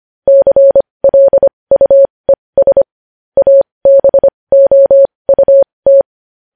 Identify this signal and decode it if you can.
Morse code, heard as a keyed tone, about 25 wpm, spaced as for 20 wpm, 550 Hz. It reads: CLUES ABOUT
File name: podcast / News headlines-1x-25x20